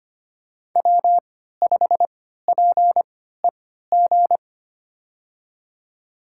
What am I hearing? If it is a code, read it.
Morse code, heard as a keyed tone, about 25 wpm, spaced as for 14 wpm, 700 Hz. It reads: W5PEG
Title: USA Call Signs - 3x Character Spacing - 25wpm